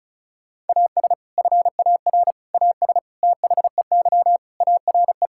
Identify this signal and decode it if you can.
Morse code, heard as a keyed tone, about 36 wpm, 700 Hz. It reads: AS FAR AS THEY ARE
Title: Sentences from Top 1000 Words - Review Part 6 - 35wpm